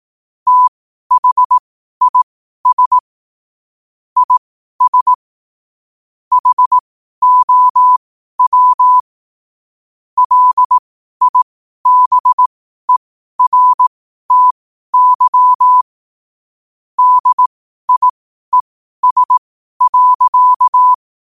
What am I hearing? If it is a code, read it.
Morse code, heard as a keyed tone, about 18 wpm, spaced as for 12 wpm, 1000 Hz. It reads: THIS IS HOW LIBERTY DIES.
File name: qotdfiles / 12wpm